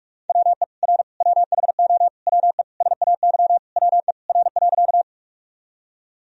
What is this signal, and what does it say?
Morse code, heard as a keyed tone, about 46 wpm, 700 Hz. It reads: WE R WHO WE SAY WE R.
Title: Sentences from Top 200 Words and Abbreviations 45wpm